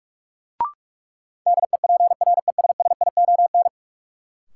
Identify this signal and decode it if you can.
Morse code, heard as a keyed tone, about 44 wpm, 700 Hz. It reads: DEPRESSION
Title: Sentences from Top 1300 Words - New Word Drill Alt - 45wpm